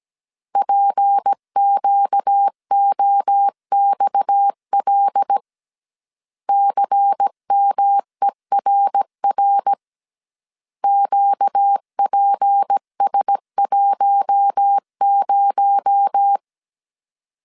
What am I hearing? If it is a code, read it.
Morse code, heard as a keyed tone, about 17 wpm, 800 Hz. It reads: PQOXL CMERR QPS10
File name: CW / DET / 1 6 1500C S A 16wpm 2635